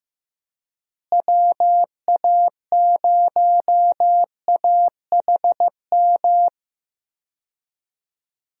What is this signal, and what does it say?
Morse code, heard as a keyed tone, about 15 wpm, 700 Hz. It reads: WA0AHM
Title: USA Call Signs - 6 Characters - Repeated 3x - 15wpm